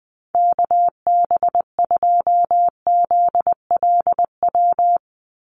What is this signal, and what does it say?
Morse code, heard as a keyed tone, about 20 wpm, 700 Hz. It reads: KB2ZLW
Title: USA Call Signs 20wpm